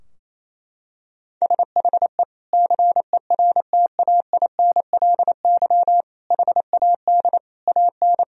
Morse code, heard as a keyed tone, about 28 wpm, 700 Hz. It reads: SHE CERTAINLY HAD AN
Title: Sentences from Top 1000 Words - Review Part 4 - 28wpm